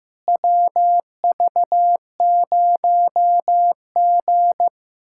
Morse code, heard as a keyed tone, about 15 wpm, 700 Hz. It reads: WV0G
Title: USA Callsign DE Callsign - 4 Characters - 15wpm